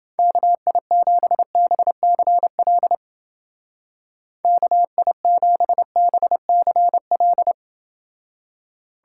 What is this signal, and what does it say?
Morse code, heard as a keyed tone, about 30 wpm, 700 Hz. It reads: KI7BCL KI7BCL